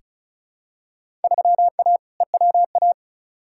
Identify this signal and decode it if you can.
Morse code, heard as a keyed tone, about 36 wpm, 700 Hz. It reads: 3A EWA
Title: Field Day Exchanges Encore - 35wpm